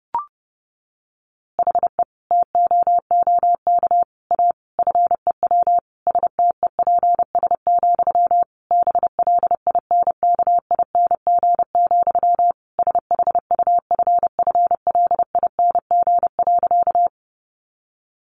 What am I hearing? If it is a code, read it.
Morse code, heard as a keyed tone, about 30 wpm, 700 Hz. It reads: HE TOOK A FEW STEPS, BLINKING, SHUFFLING.
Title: A Sound Of Thunder 30wpm